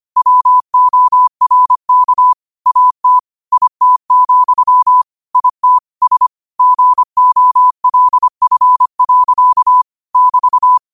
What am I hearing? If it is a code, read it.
Morse code, heard as a keyed tone, about 25 wpm, 1000 Hz. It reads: WORK AT IT, IT S GOLF. <BT>